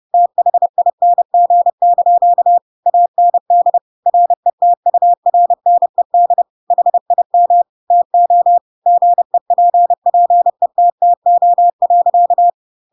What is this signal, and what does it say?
Morse code, heard as a keyed tone, about 30 wpm, 700 Hz. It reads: THING) AND RETURNED HIM TO GEPPETTO.
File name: Pinocchio / Pinocchio ch3 30 30 ik1hge